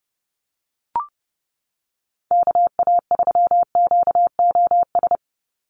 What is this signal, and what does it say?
Morse code, heard as a keyed tone, about 30 wpm, 700 Hz. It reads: KA3QOS
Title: USA Call Signs 30wpm